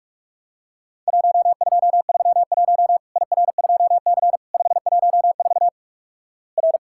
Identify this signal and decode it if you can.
Morse code, heard as a keyed tone, about 48 wpm, 700 Hz. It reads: 1231 IR2C 514 R
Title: Worked All Europe DX Contest - QTC Exchange - 45wpm